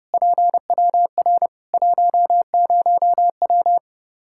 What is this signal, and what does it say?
Morse code, heard as a keyed tone, about 29 wpm, 700 Hz. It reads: PWR 10W
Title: Practice QSO - Exchange 2 - Info - 30wpm